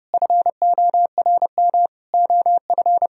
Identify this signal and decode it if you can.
Morse code, heard as a keyed tone, about 30 wpm, 700 Hz. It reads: FORM OF
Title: Sentences from Top 1000 Words - Review Part 3 - 30wpm